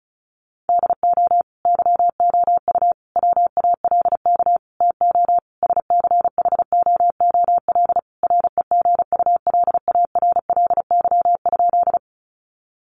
Morse code, heard as a keyed tone, about 34 wpm, 700 Hz. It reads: DO YOU WALK TO SCHOOL REGULARLY?
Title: Sentences from Top 1400 Words 35wpm